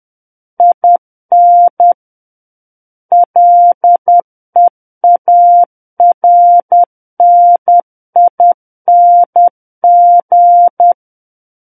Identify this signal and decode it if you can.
Morse code, heard as a keyed tone, about 10 wpm, 700 Hz. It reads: IN LEARNING
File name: Book One / 01-Chapter 1 - 10 WPM